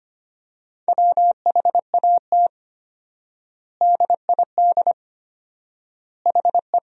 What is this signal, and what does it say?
Morse code, heard as a keyed tone, about 25 wpm, 700 Hz. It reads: WHAT DID HE
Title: 4x Word-Spacing - Sentences from Top 200 Words - 25wpm